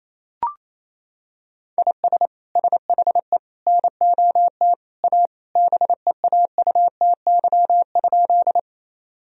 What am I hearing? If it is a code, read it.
Morse code, heard as a keyed tone, about 27 wpm, 700 Hz. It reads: IS SHE NOT A BEAUTY?
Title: Sentences from Top 1000 Words - Review Part 2 - 28wpm